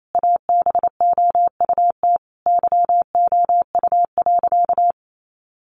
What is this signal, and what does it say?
Morse code, heard as a keyed tone, about 28 wpm, 700 Hz. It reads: ABOUT YOU.
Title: Sentences from Top 400 Words 28wpm